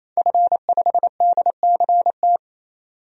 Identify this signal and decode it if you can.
Morse code, heard as a keyed tone, about 27 wpm, 700 Hz. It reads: F5DCT
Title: DXCC-100 Call Signs - 5 Characters - 28wpm